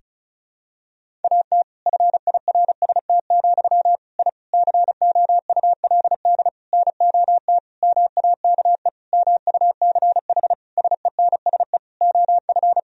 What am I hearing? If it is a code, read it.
Morse code, heard as a keyed tone, about 36 wpm, 700 Hz. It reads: AT FIRST, I COULD NOT MAKE MUCH SENSE OF
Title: Sentences from Top 1000 Words - Review Part 4 - 35wpm